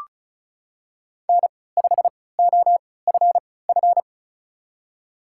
Morse code, heard as a keyed tone, about 36 wpm, spaced as for 20 wpm, 690 Hz. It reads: N5OFF
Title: USA Call Signs - 5 Characters - 3x Character Spacing - 35wpm